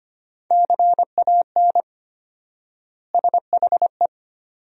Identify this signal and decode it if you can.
Morse code, heard as a keyed tone, about 25 wpm, 700 Hz. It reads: CAN SHE
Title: 4x Word-Spacing - Sentences from Top 400 Words - 25wpm